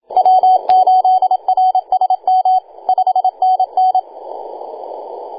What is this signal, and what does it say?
Morse code, heard as a keyed tone, about 27 wpm, 750 Hz, with radio noise. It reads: W8RSM 5NN
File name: e6gg-18-083-cw